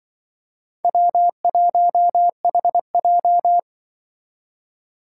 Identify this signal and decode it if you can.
Morse code, heard as a keyed tone, about 25 wpm, 700 Hz. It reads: W1HJ
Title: Speed Racing - USA 1x2 Call Signs 20wpm